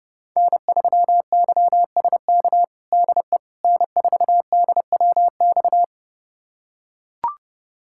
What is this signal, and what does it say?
Morse code, heard as a keyed tone, about 30 wpm, 700 Hz. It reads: N3YSK DE N4DWX
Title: USA Callsign DE Callsign - 5 Characters - 30wpm